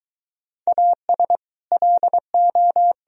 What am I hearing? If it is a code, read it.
Morse code, heard as a keyed tone, about 24 wpm, 700 Hz. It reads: AS LO